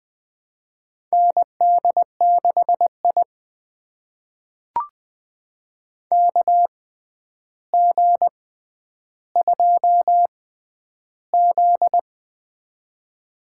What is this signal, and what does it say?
Morse code, heard as a keyed tone, about 20 wpm, spaced as for 10 wpm, 700 Hz. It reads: ND6I KG2Z
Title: USA Call Signs - 4 Characters - 6x Character Spacing - 20wpm